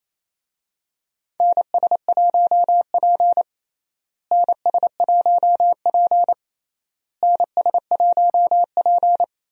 Morse code, heard as a keyed tone, about 28 wpm, 700 Hz. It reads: NS1P NS1P NS1P
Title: USA Call Signs - 4 Characters - Repeated 3x - 28wpm